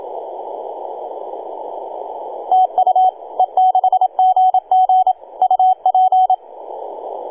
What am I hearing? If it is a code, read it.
Morse code, heard as a keyed tone, about 27 wpm, 750 Hz, with radio noise. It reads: TU E6GG UP
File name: e6gg-18-083-cw